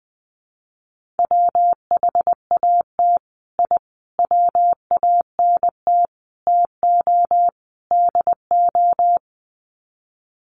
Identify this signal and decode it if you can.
Morse code, heard as a keyed tone, about 20 wpm, 700 Hz. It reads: WHAT I WANT TO DO
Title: Sets of 5 Words Encore 20wpm